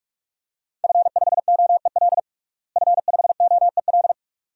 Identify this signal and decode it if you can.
Morse code, heard as a keyed tone, about 46 wpm, 700 Hz. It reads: F5OEL F5OEL
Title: DXCC-100 Call Signs - 5 Characters - Repeated 2x - 45wpm